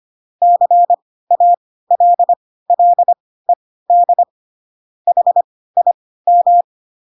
Morse code, heard as a keyed tone, about 25 wpm, spaced as for 16 wpm, 700 Hz. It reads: CALLED HIM
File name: Pinocchio / Pinocchio ch1 10 25 ik1hge